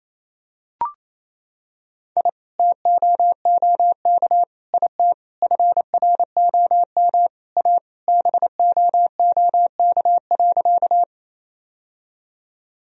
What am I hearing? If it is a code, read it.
Morse code, heard as a keyed tone, about 28 wpm, 700 Hz. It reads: I TOOK IT FROM A BOOK.
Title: Sentences from Top 1000 Words - Review Part 1 - 28wpm